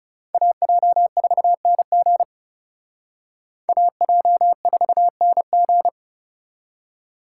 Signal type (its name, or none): Morse code